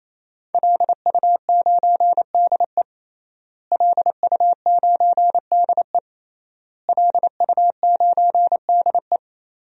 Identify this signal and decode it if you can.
Morse code, heard as a keyed tone, about 28 wpm, 700 Hz. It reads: LU9DE LU9DE LU9DE
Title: DXCC-100 Call Signs - 5 Characters - Repeated 3x - 28wpm